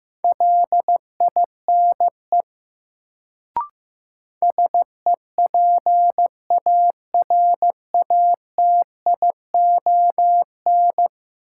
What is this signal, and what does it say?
Morse code, heard as a keyed tone, about 15 wpm, 700 Hz. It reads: LINE SEPARATION
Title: Sets of 5 Words - New Word Drill - 15wpm